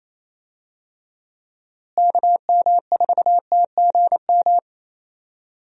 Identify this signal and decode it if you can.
Morse code, heard as a keyed tone, about 28 wpm, 700 Hz. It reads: KM4TGM
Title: USA Call Signs - 6 Characters - Repeated 2x - 28wpm